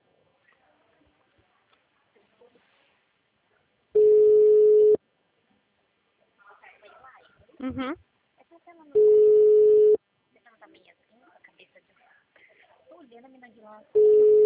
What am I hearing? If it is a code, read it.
Ringback tone (Europe, 425 Hz), with speech.